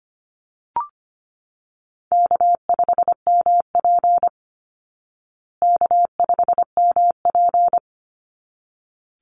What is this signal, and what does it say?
Morse code, heard as a keyed tone, about 25 wpm, 700 Hz. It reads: K5MP K5MP
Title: Easy Call Signs Repeated 2x - 25wpm